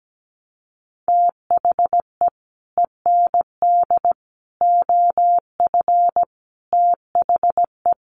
Morse code, heard as a keyed tone, about 17 wpm, 700 Hz. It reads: THE END OF THE
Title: Sets of 5 Words Encore 17wpm